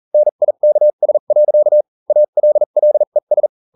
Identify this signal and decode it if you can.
Morse code, heard as a keyed tone, about 40 wpm, spaced as for 35 wpm, 600 Hz. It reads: NIKS. ALLES